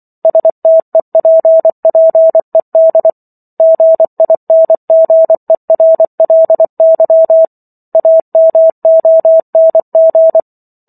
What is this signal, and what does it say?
Morse code, heard as a keyed tone, about 24 wpm, 650 Hz. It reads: STEPPED GINGERLY AMONG